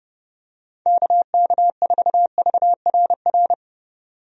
Morse code, heard as a keyed tone, about 30 wpm, 700 Hz. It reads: KK4VRR
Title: USA Call Signs - 6 Characters - 30wpm